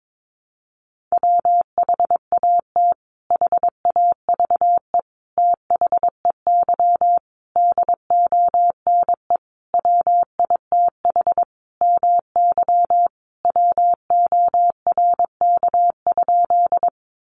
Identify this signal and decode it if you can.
Morse code, heard as a keyed tone, about 22 wpm, 700 Hz. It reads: WHAT HAVE THEY DONE WITH MY WORK?
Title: Sentences from Top 500 Words 22wpm